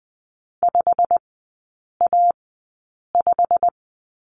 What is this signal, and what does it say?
Morse code, heard as a keyed tone, about 20 wpm, spaced as for 8 wpm, 700 Hz. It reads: 5A5
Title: 087 Lesson 15e - Number 5 - Three Characters 20wpm